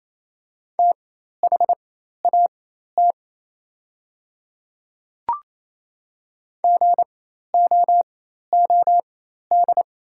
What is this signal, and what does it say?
Morse code, heard as a keyed tone, about 28 wpm, spaced as for 13 wpm, 700 Hz. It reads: THAT GOOD
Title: 4x Character Spacing - Top 300 Words - 28wpm